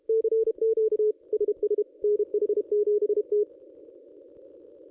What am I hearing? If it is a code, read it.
Morse code, heard as a keyed tone, about 28 wpm, 440 Hz. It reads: CQ SS NH7T
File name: contests / ss-cw-k5tr-op-n5rz -1200